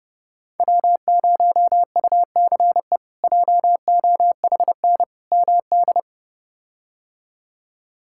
Morse code, heard as a keyed tone, about 30 wpm, 700 Hz. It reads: W0UCE JOHN MD
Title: K1USN SST QSO Practice - 30wpm